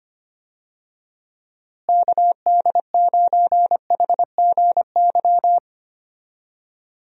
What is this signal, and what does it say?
Morse code, heard as a keyed tone, about 25 wpm, 700 Hz. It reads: KD9HGY
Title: USA Call Signs - 6 Characters - Repeated 2x - 25wpm